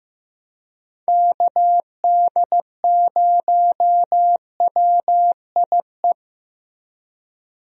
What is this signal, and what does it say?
Morse code, heard as a keyed tone, about 15 wpm, 700 Hz. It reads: KD0WIE
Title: USA Call Signs - 6 Characters - Repeated 2x - 15wpm